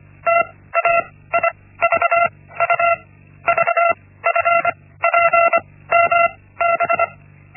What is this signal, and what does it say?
Morse code, heard as a keyed tone, about 25 wpm, spaced as for 16 wpm, 680 Hz. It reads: TAIVU VFPMB
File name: Linksmakalnis-1